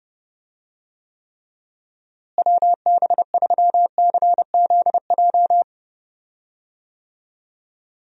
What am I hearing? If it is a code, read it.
Morse code, heard as a keyed tone, about 30 wpm, 700 Hz. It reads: WB3CZJ
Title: USA Call Signs - 6 Characters - Repeated 2x - 30wpm